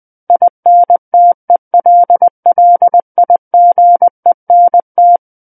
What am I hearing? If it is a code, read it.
Morse code, heard as a keyed tone, about 20 wpm, 700 Hz. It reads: INTELLIGENT
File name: 20 WPM